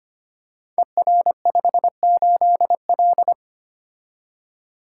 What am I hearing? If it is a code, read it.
Morse code, heard as a keyed tone, about 25 wpm, 700 Hz. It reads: ER58L
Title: Code Groups - 5 Letter:Numbers - 25wpm